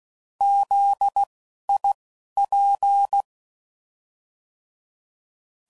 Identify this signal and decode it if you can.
Morse code, heard as a keyed tone, about 16 wpm, spaced as for 11 wpm, 800 Hz. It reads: ZIP
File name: Prefixes / By First Letter / Zs